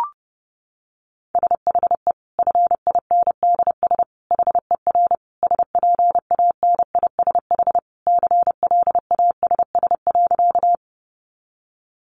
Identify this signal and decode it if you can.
Morse code, heard as a keyed tone, about 30 wpm, 700 Hz. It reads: SHE FINDS HER SPANISH CLASS.
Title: Additional Easy Phrases 30wpm